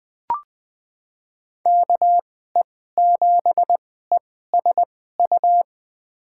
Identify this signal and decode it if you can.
Morse code, heard as a keyed tone, about 20 wpm, spaced as for 14 wpm, 700 Hz. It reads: KE7ESU
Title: USA Call Signs - 6 Characters - 2x Character Spacing - 20wpm